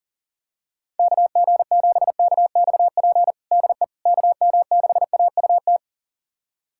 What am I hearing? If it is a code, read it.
Morse code, heard as a keyed tone, about 40 wpm, 700 Hz. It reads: KG7KXP DE KM6AUT
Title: USA Callsign DE Callsign - 6 Characters - 40wpm